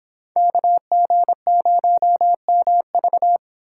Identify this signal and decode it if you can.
Morse code, heard as a keyed tone, about 26 wpm, 700 Hz. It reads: KG0MV